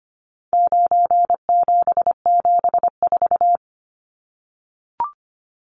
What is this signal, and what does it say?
Morse code, heard as a keyed tone, about 25 wpm, 700 Hz. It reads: 9774
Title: Numbers - 4-Digit 25wpm